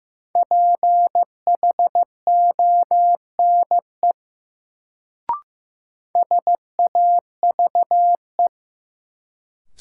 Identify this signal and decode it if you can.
Morse code, heard as a keyed tone, about 15 wpm, 700 Hz. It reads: PHONE SAVE
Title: Sentences from Top 1100 Words - New Word Drill - 15wpm